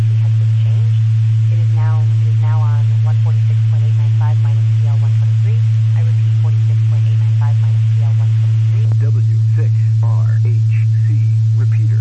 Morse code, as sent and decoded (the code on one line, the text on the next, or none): none